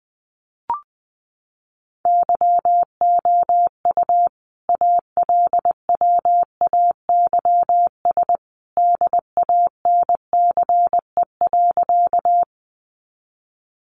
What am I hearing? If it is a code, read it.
Morse code, heard as a keyed tone, about 19 wpm, 700 Hz. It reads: YOU ALWAYS DANCE.
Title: Sentences from Top 700 Words 20wpm